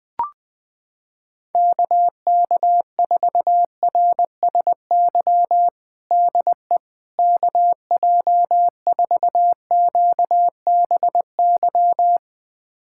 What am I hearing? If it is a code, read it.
Morse code, heard as a keyed tone, about 20 wpm, 700 Hz. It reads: KK4RSY DE KJ4QBY
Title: USA Callsign DE Callsign - 6 Characters - 20wpm